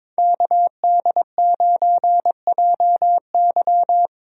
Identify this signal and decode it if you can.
Morse code, heard as a keyed tone, about 22 wpm, 700 Hz. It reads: KD9JY
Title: USA Call Signs - 5 Characters - Repeated 2x - 22wpm